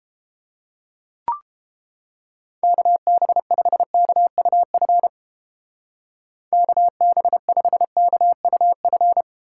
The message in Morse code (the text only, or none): KB5KUF KB5KUF